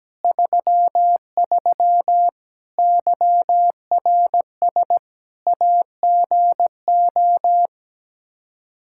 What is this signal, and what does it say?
Morse code, heard as a keyed tone, about 17 wpm, 700 Hz. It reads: 33 YRS AGO
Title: K1USN SST QSO Practice 17wpm